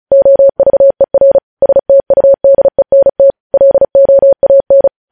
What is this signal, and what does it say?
Morse code, heard as a keyed tone, about 36 wpm, 550 Hz. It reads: OVER STUDENT LOAN
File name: podcast / News headlines-1x-35x35